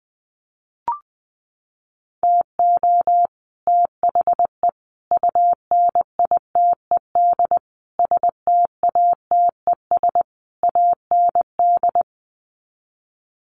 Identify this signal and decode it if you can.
Morse code, heard as a keyed tone, about 20 wpm, 700 Hz. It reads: TO THE UNITED STATES AND
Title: Sets of 5 Words Encore 20wpm